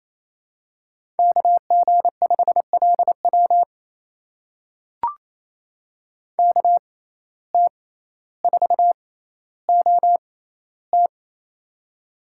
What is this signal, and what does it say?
Morse code, heard as a keyed tone, about 27 wpm, spaced as for 14 wpm, 700 Hz. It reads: KG5LW KT4OT
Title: USA Call Signs - 5 Characters - 6x Character Spacing - 28wpm